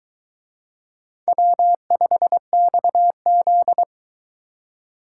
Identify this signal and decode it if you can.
Morse code, heard as a keyed tone, about 21 wpm, 700 Hz. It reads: W5XZ